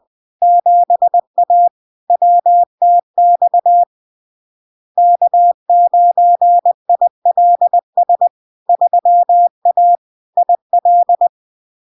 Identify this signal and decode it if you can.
Morse code, heard as a keyed tone, about 20 wpm, 700 Hz. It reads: 7A WTX K9ILS 3A IL
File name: LARK-FD-2022-Log-in-CW-20wpm-700-Hz